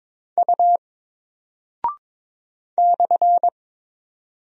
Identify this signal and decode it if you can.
Morse code, heard as a keyed tone, about 22 wpm, 700 Hz. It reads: U /
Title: Single Character (Letter:Number:Punctuation:Prosign) Alt 22wpm